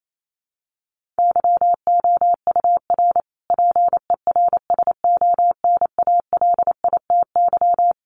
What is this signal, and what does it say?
Morse code, heard as a keyed tone, about 28 wpm, 700 Hz. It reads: YOUR PERSONALITY
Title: Sentences from Top 1400 Words 28wpm